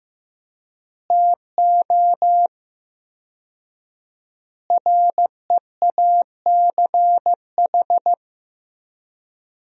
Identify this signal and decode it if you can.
Morse code, heard as a keyed tone, about 16 wpm, 700 Hz. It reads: TO REACH
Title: 4x Word-Spacing - Sentences from Top 400 Words - 15wpm